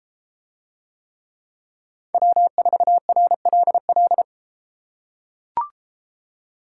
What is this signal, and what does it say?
Morse code, heard as a keyed tone, about 32 wpm, 700 Hz. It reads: W4RLL